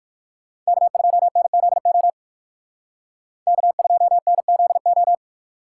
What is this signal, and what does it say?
Morse code, heard as a keyed tone, about 58 wpm, 700 Hz. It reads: K2NZO K2NZO